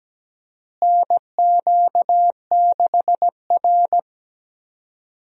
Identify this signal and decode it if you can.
Morse code, heard as a keyed tone, about 17 wpm, 700 Hz. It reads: NQ6R
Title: USA Call Signs - 4 Characters - Repeated 2x - 17wpm